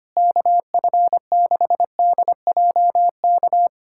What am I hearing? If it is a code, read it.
Morse code, heard as a keyed tone, about 25 wpm, 700 Hz. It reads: KF6DJK
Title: USA Callsign DE Callsign - 6 Characters - 25wpm